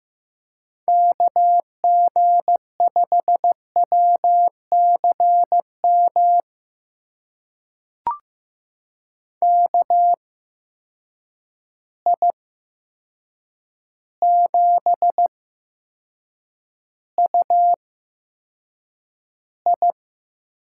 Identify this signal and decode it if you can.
Morse code, heard as a keyed tone, about 15 wpm, spaced as for 7 wpm, 700 Hz. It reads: KG5WCM KI7UI
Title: USA Call Signs - 6 Characters - 8x Character Spacing - 15wpm